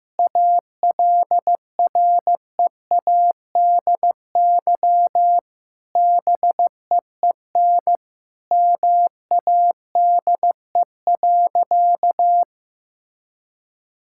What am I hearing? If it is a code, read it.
Morse code, heard as a keyed tone, about 16 wpm, 700 Hz. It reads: ALREADY BEEN MADE.
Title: Sentences from Top 1000 Words - Review Part 5 - 15wpm